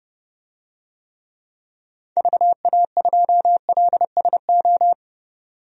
Morse code, heard as a keyed tone, about 30 wpm, 700 Hz. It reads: VA2LSO
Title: DXCC-100 Call Signs - 6 Characters - 30wpm